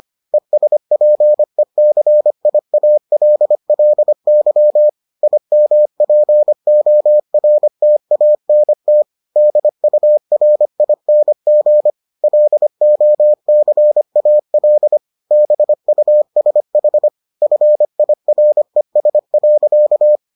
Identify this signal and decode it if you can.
Morse code, heard as a keyed tone, about 25 wpm, 600 Hz. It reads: ESPECIALLY IMPORTANT DURING LOCAL BUSH FIRES.